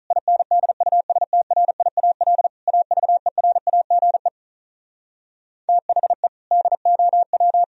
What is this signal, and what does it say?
Morse code, heard as a keyed tone, about 46 wpm, 700 Hz. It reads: INDUSTRIAL AVERAGE THE DOW